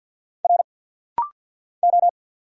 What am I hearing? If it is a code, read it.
Morse code, heard as a keyed tone, about 49 wpm, 700 Hz. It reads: R O
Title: Single Letter-Number - Rapid-Fire 50wpm